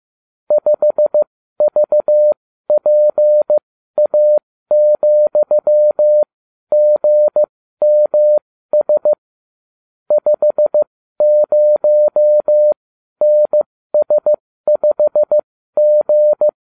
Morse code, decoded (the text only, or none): 5VPA,GMS 50NS5G